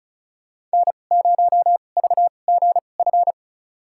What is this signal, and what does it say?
Morse code, heard as a keyed tone, about 36 wpm, spaced as for 25 wpm, 700 Hz. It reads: N0VGF
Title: USA Call Signs - 2x Character Spacing - 35wpm